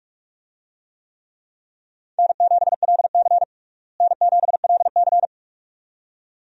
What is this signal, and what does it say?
Morse code, heard as a keyed tone, about 46 wpm, 700 Hz. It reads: N7LC N7LC
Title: USA Call Signs - 4 Characters - Repeated 2x - 45wpm